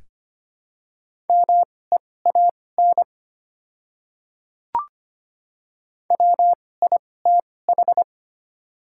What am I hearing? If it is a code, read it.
Morse code, heard as a keyed tone, about 25 wpm, spaced as for 18 wpm, 700 Hz. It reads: MEAN WITH
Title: 2x Character Spacing - Top 300 Words - 25wpm